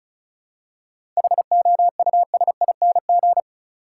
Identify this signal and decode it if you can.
Morse code, heard as a keyed tone, about 34 wpm, 700 Hz. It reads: HOUSING
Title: Seven Letter Words - 35wpm